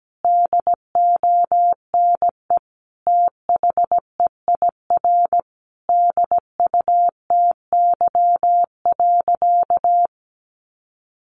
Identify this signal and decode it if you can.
Morse code, heard as a keyed tone, about 17 wpm, 700 Hz. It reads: DONE THEIR DUTY.
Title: Sentences from Top 1500 Words 17wpm